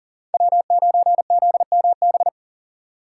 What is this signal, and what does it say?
Morse code, heard as a keyed tone, about 40 wpm, 700 Hz. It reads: W9ZMB
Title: USA Call Signs - 5 Characters - 3x Character Spacing - 40wpm